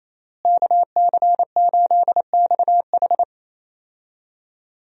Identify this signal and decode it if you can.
Morse code, heard as a keyed tone, about 28 wpm, 700 Hz. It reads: KC8XH
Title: USA Call Signs - 5 Characters - Repeated 3x - 28wpm